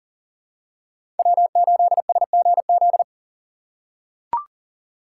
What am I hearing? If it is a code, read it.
Morse code, heard as a keyed tone, about 40 wpm, 700 Hz. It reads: W8SGZ
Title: USA Call Signs - 4x Character Spacing - 40wpm